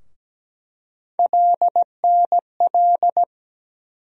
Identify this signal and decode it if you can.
Morse code, heard as a keyed tone, about 17 wpm, 700 Hz. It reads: LNL
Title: USA Call Sign Postfixes - 3 Letters 17wpm